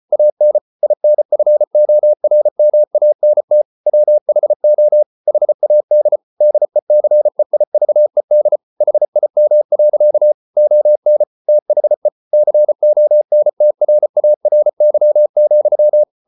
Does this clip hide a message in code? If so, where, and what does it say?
Yes: Morse code from 0.1 to 16.0 s, heard as a keyed tone, about 34 wpm, 600 Hz. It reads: AN INFORMANT WHO HAD DECEIVED HIM. ON THE CONTRARY,